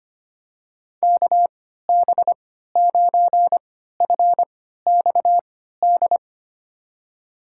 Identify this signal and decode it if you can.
Morse code, heard as a keyed tone, about 25 wpm, spaced as for 14 wpm, 700 Hz. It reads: KB9FXD
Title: USA Call Signs - 6 Characters - 3x Character Spacing - 25wpm